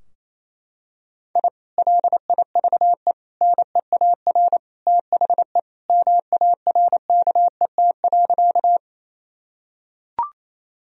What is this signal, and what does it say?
Morse code, heard as a keyed tone, about 28 wpm, 700 Hz. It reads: I LIVE NEAR THE MARKET.
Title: Sentences from Top 1000 Words - Review Part 3 - 28wpm